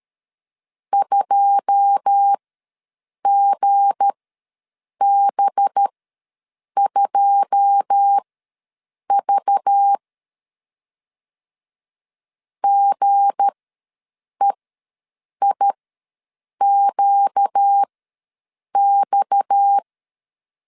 Morse code, heard as a keyed tone, about 13 wpm, spaced as for 7 wpm, 800 Hz. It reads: 2GB2V GEIQX
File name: LEZIONI / 9 2 ws 12wpm 3406